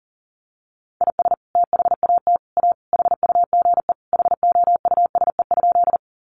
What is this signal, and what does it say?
Morse code, heard as a keyed tone, about 40 wpm, 700 Hz. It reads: IS THAT A HUGE HOUSE?
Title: Sentences from Top 1100 Words 40wpm